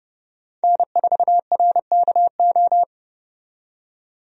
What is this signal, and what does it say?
Morse code, heard as a keyed tone, about 30 wpm, 700 Hz. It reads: N4RKO
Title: USA Call Signs - 5 Characters - 30wpm